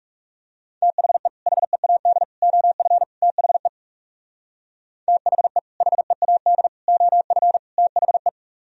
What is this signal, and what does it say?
Morse code, heard as a keyed tone, about 37 wpm, 700 Hz. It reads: THE HEAD OF THE THE HEAD OF THE
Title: Speed-Racing - Sets of 4 Words 30wpm